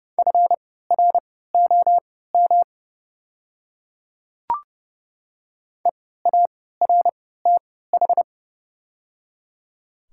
Morse code, heard as a keyed tone, about 31 wpm, spaced as for 17 wpm, 700 Hz. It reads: FROM EARTH
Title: 3x Character Spacing - Top 300 Words - 30wpm